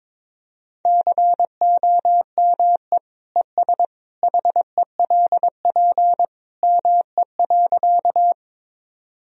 Morse code, heard as a keyed tone, about 22 wpm, 700 Hz. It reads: COME ES HELP ME.
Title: Sentences from Top 200 Words and Abbreviations 22wpm